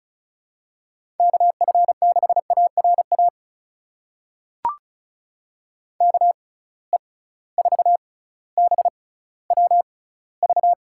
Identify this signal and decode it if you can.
Morse code, heard as a keyed tone, about 34 wpm, spaced as for 18 wpm, 700 Hz. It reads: KF6ARA KE4BWV
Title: USA Call Signs - 6 Characters - 6x Character Spacing - 35wpm